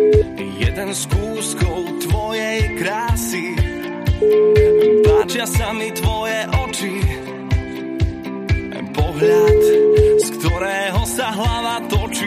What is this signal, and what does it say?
Ringback tone (Europe, 425 Hz), with noise.